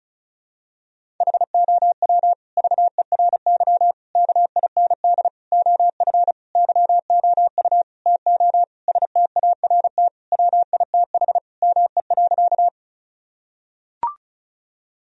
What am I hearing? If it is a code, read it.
Morse code, heard as a keyed tone, about 36 wpm, 700 Hz. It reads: HOW VERY KIND OF YOU TO START WITH ME.
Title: Sentences from Top 1000 Words - Review Part 1 - 35wpm